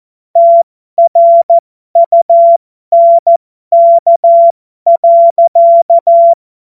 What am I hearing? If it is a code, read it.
Morse code, heard as a keyed tone, about 14 wpm, spaced as for 12 wpm, 680 Hz. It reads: TRUNK.